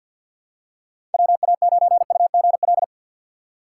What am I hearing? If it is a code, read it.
Morse code, heard as a keyed tone, about 49 wpm, 700 Hz. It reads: WA9UGL
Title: USA Call Signs - 4x Character Spacing - 50wpm